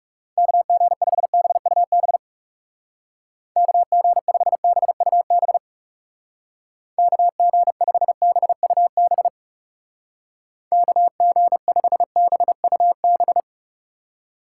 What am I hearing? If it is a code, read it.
Morse code, heard as a keyed tone, about 34 wpm, 700 Hz. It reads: KG5BUB KG5BUB KG5BUB KG5BUB
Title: Speed Racing - USA Call Signs 30wpm